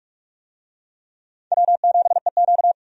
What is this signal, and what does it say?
Morse code, heard as a keyed tone, about 41 wpm, 700 Hz. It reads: W7EQ
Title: USA Call Signs - 4 Characters - 45wpm